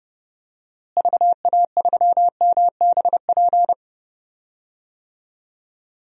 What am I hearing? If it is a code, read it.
Morse code, heard as a keyed tone, about 30 wpm, 700 Hz. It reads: VA3MBP
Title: DXCC-100 Call Signs - 6 Characters - Repeated 2x - 30wpm